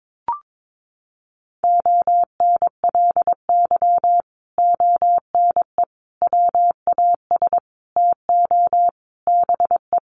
Morse code, heard as a keyed tone, about 22 wpm, 700 Hz. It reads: ONLY ONE WAS TO BE
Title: Sentences from Top 1400 Words 22wpm